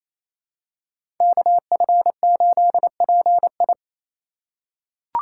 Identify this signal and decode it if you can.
Morse code, heard as a keyed tone, about 28 wpm, 700 Hz. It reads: KF8PI
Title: USA Call Signs - 5 Characters - 2x Character Spacing - 28wpm